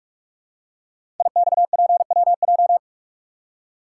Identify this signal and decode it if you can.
Morse code, heard as a keyed tone, about 44 wpm, 700 Hz. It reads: IXPWJ
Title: Code Groups - 5 Letters - 45wpm